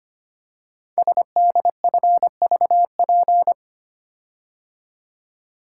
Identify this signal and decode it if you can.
Morse code, heard as a keyed tone, about 25 wpm, 700 Hz. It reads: SDFVP
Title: Code Groups - 5 Letter:Numbers - 25wpm